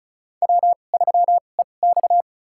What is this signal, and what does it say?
Morse code, heard as a keyed tone, about 34 wpm, spaced as for 25 wpm, 700 Hz. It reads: W3EX
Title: USA Call Signs - 4 Characters - 2x Character Spacing - 35wpm